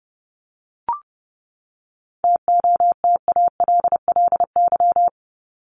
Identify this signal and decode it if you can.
Morse code, heard as a keyed tone, about 30 wpm, 700 Hz. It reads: TOTALLY
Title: LY Suffix - Top 100 Words with Base Words 30wpm